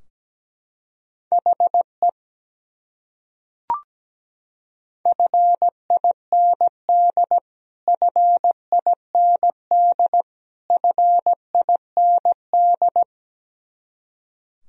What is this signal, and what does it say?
Morse code, heard as a keyed tone, about 17 wpm, 700 Hz. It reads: HE FIND FIND FIND
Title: Top 100 Words Repeated 17wpm